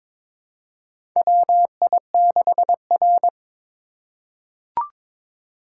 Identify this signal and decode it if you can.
Morse code, heard as a keyed tone, about 22 wpm, 700 Hz. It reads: WI6R
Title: USA Call Signs - 4 Characters - 4x Character Spacing - 22wpm